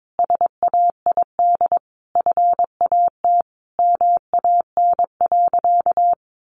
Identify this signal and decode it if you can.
Morse code, heard as a keyed tone, about 22 wpm, 700 Hz. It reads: SAID FAT MAN.
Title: Sentences from Top 1100 Words 22wpm